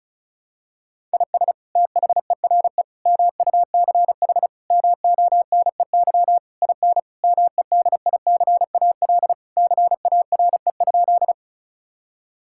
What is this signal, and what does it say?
Morse code, heard as a keyed tone, about 36 wpm, 700 Hz. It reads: IS THERE MUCH MONEY IN MEDICAL CARE?
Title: Sentences from Top 1000 Words - Review Part 5 - 35wpm